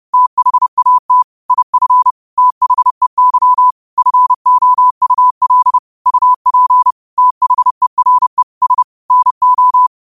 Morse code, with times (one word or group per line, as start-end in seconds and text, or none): 0.1-1.2 THAT
1.5-2.1 IF
2.4-3.7 THEY
4.0-5.8 FOUL
6.1-6.9 UP
7.2-8.4 THERE
8.6-8.8 S
9.1-9.9 NO